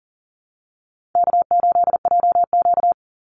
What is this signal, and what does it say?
Morse code, heard as a keyed tone, about 40 wpm, 700 Hz. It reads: K8JQ
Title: Easy Call Signs - 40wpm